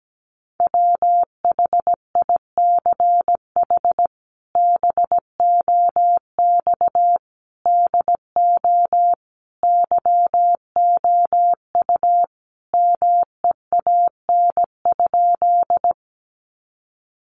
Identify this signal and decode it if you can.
Morse code, heard as a keyed tone, about 17 wpm, 700 Hz. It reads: WHICH BOX DO YOU MEAN?
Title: Sentences from Top 500 Words 17wpm